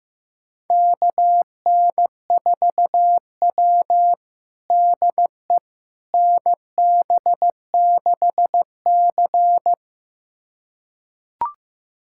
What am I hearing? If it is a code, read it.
Morse code, heard as a keyed tone, about 16 wpm, 700 Hz. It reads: KN4W DE NB6C
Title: USA Callsign DE Callsign - 4 Characters - 15wpm